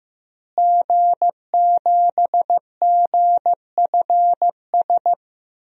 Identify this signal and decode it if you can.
Morse code, heard as a keyed tone, about 15 wpm, 700 Hz. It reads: G7GFS
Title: DXCC-100 Call Signs - 5 Characters - Repeated 2x - 15wpm